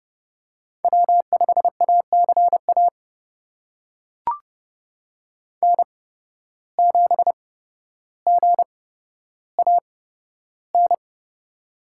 Morse code, heard as a keyed tone, about 31 wpm, spaced as for 13 wpm, 700 Hz. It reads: W5ACA N7GAN
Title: USA Call Signs - 5 Characters - 8x Character Spacing - 30wpm